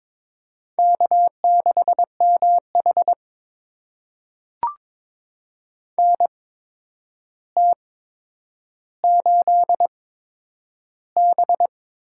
Morse code, heard as a keyed tone, about 22 wpm, spaced as for 9 wpm, 700 Hz. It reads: K6MH NT8B
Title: USA Call Signs - 4 Characters - 8x Character Spacing - 22wpm